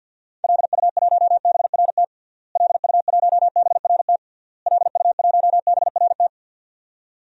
Morse code, heard as a keyed tone, about 48 wpm, 700 Hz. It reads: LU1BRT LU1BRT LU1BRT
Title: DXCC-100 Call Signs - 6 Characters - Repeated 3x - 50wpm